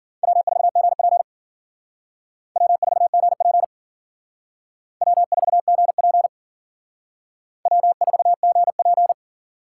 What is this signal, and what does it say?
Morse code, heard as a keyed tone, about 50 wpm, 700 Hz. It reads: W4GP W4GP W4GP W4GP